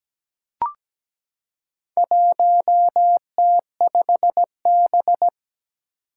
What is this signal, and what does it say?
Morse code, heard as a keyed tone, about 17 wpm, 700 Hz. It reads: 1T5B
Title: Four Challenging Letter-Number Combinations 17wpm